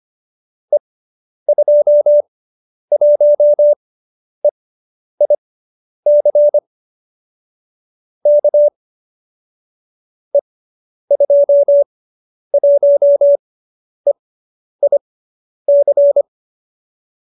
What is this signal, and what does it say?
Morse code, heard as a keyed tone, about 25 wpm, spaced as for 10 wpm, 590 Hz. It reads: E21EIC K E21EIC